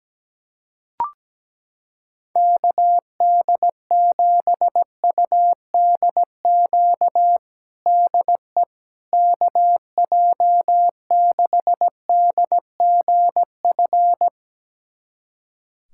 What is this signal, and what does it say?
Morse code, heard as a keyed tone, about 17 wpm, 700 Hz. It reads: KD7UDQ DE KJ6DGF
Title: USA Callsign DE Callsign - 6 Characters - 17wpm